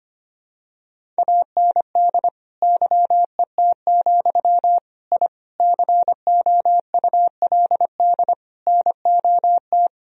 Morse code, heard as a keyed tone, about 25 wpm, 700 Hz. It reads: AND YET, I COULD NOT
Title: Sentences from Top 1000 Words - Review Part 1 - 25wpm